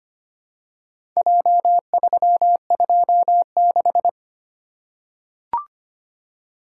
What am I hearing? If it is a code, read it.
Morse code, heard as a keyed tone, about 25 wpm, 700 Hz. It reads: J326
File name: practice / 25wpm / Four Letter-Number Combinations - 25wpm